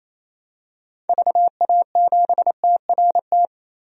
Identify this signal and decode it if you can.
Morse code, heard as a keyed tone, about 28 wpm, 700 Hz. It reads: VA7TRT
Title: DXCC-100 Call Signs - 6 Characters - 28wpm